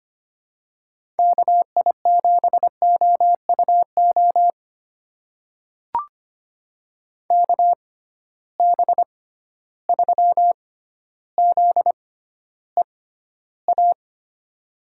Morse code, heard as a keyed tone, about 25 wpm, spaced as for 13 wpm, 700 Hz. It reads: KI7OUO KB3ZEA
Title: USA Call Signs - 6 Characters - 6x Character Spacing - 25wpm